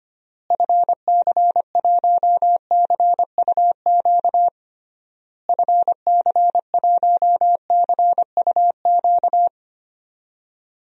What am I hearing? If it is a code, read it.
Morse code, heard as a keyed tone, about 25 wpm, 700 Hz. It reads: FC1CUQ FC1CUQ
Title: DXCC-100 Call Signs - 6 Characters - Repeated 3x - 25wpm